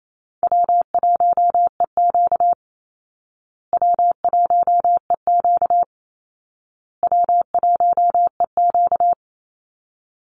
Morse code, heard as a keyed tone, about 28 wpm, 700 Hz. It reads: W1EQ W1EQ W1EQ
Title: Easy Call Signs Repeated 3x - 28wpm